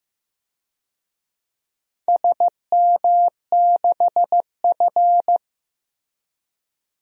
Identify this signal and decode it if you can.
Morse code, heard as a keyed tone, about 15 wpm, 700 Hz. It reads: SM6F
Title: DXCC-100 Call Signs - 4 Characters - Repeated 3x - 15wpm